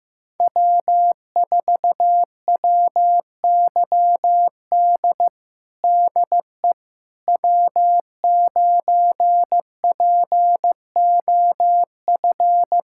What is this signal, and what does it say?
Morse code, heard as a keyed tone, about 15 wpm, 700 Hz. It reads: W4WYD DE W9POF
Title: USA Callsign DE Callsign - 5 Characters - 15wpm